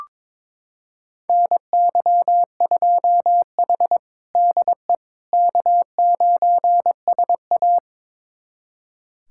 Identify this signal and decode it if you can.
Morse code, heard as a keyed tone, about 22 wpm, 700 Hz. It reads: NY2H DE K9SA
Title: USA Callsign DE Callsign - 4 Characters - 22wpm